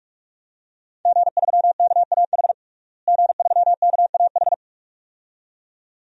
Morse code, heard as a keyed tone, about 46 wpm, 700 Hz. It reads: G3KAH G3KAH
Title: DXCC-100 Call Signs - 5 Characters - Repeated 2x - 45wpm